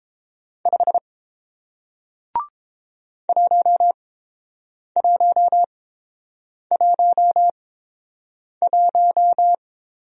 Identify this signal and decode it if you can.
Morse code, heard as a keyed tone, about 30 wpm, spaced as for 7 wpm, 700 Hz. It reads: 5 1111